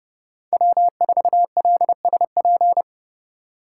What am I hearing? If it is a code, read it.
Morse code, heard as a keyed tone, about 30 wpm, 700 Hz. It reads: W4LSP
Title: USA Call Signs - 5 Characters - 2x Character Spacing - 30wpm